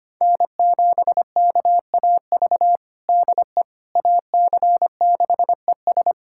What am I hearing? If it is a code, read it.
Morse code, heard as a keyed tone, about 25 wpm, 700 Hz. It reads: N7KAV DE AC6ES
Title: USA Callsign DE Callsign - 5 Characters - 25wpm